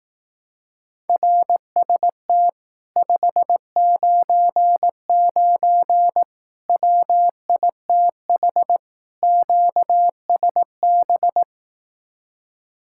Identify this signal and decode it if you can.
Morse code, heard as a keyed tone, about 18 wpm, 700 Hz. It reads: RST 599 WITH QSB
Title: Speed-Racing Common QSO Phrases Encore - 15wpm